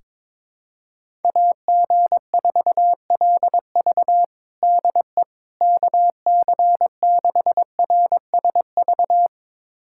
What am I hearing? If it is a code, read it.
Morse code, heard as a keyed tone, about 22 wpm, 700 Hz. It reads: AG4LV DE KC6RSV
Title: USA Callsign DE Callsign 22wpm